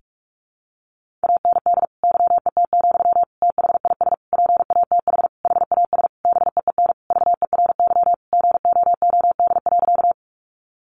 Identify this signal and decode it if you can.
Morse code, heard as a keyed tone, about 44 wpm, 700 Hz. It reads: AND YET, THIS PATH HAS BEEN VERY GOOD.
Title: Sentences from Top 1400 Words 45wpm